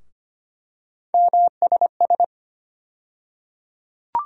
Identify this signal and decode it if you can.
Morse code, heard as a keyed tone, about 25 wpm, 700 Hz. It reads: MSS
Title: USA Call Sign Postfixes - 3 Letters 25wpm